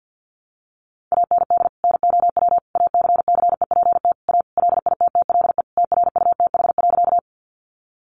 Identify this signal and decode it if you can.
Morse code, heard as a keyed tone, about 50 wpm, 700 Hz. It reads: AND NOW ACCEPT A LITTLE TRUTH.
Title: Sentences from Top 1300 Words 50wpm